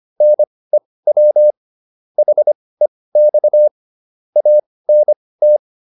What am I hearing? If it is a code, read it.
Morse code, heard as a keyed tone, about 25 wpm, spaced as for 18 wpm, 600 Hz. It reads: NEW HEX ANT
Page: intermediate-practice-files